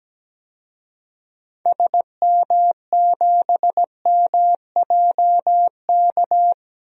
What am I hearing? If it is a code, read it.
Morse code, heard as a keyed tone, about 17 wpm, 700 Hz. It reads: SM7MJK
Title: DXCC-100 Call Signs - 6 Characters - 17wpm